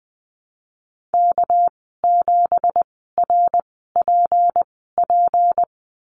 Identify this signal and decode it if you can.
Morse code, heard as a keyed tone, about 20 wpm, spaced as for 15 wpm, 700 Hz. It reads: K7RPP
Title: 231 Lesson 37g - Letter x - Call Signs 20wpm